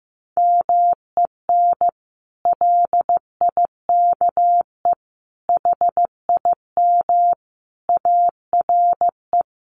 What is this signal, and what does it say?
Morse code, heard as a keyed tone, about 15 wpm, 700 Hz. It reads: MEN LIKE HIM ARE
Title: Sentences from Top 1100 Words 15wpm